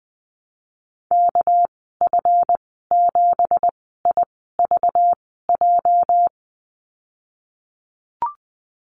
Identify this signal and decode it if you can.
Morse code, heard as a keyed tone, about 20 wpm, spaced as for 15 wpm, 700 Hz. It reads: KF7IVJ
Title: 213 Lesson 34g - Number 8 - Call Signs 20wpm